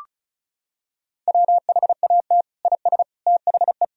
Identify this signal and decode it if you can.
Morse code, heard as a keyed tone, about 36 wpm, 700 Hz. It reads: WHAT IS THE
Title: Sentences from Top 1000 Words - Review Part 4 - 35wpm